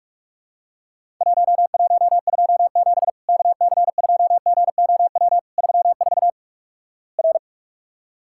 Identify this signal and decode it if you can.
Morse code, heard as a keyed tone, about 44 wpm, 700 Hz. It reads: 1127 KC2GOW 34 R
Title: Worked All Europe DX Contest - QTC Exchange - 45wpm